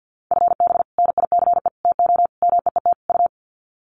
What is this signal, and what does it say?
Morse code, heard as a keyed tone, about 50 wpm, 700 Hz. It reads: FB NICE TO MEET U
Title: Common QSO Phrases 50wpm